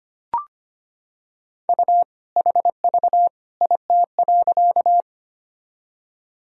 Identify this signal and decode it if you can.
Morse code, heard as a keyed tone, about 25 wpm, 700 Hz. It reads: U HV IT.
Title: Sentences from Top 100 Words and Abbreviations 25wpm